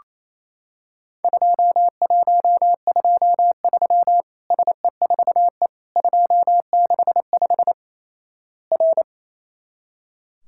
Morse code, heard as a keyed tone, about 28 wpm, 700 Hz. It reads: 2123 SE4E 265 R
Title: Worked All Europe DX Contest - QTC Exchange - 28wpm